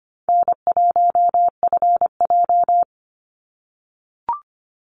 Morse code, heard as a keyed tone, about 25 wpm, 700 Hz. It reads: N1FJ
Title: NA SOTA Chaser Callsigns 25wpm